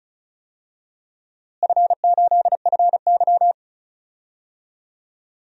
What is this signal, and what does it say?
Morse code, heard as a keyed tone, about 36 wpm, 700 Hz. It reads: F8FY
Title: DXCC-100 Call Signs - 4 Characters - Repeated 3x - 35wpm